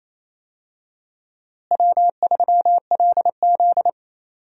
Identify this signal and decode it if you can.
Morse code, heard as a keyed tone, about 28 wpm, 700 Hz. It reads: W3LZ
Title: USA Call Signs - 4 Characters - Repeated 3x - 28wpm